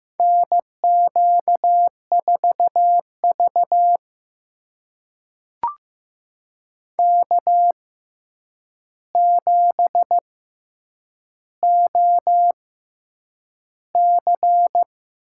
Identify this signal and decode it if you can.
Morse code, heard as a keyed tone, about 15 wpm, spaced as for 8 wpm, 700 Hz. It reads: NQ4V K7OC
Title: USA Call Signs - 4 Characters - 6x Character Spacing - 15wpm